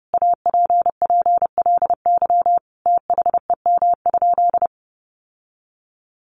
Morse code, heard as a keyed tone, about 30 wpm, 700 Hz. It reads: APPLY THEM?
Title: Sentences from Top 1400 Words 30wpm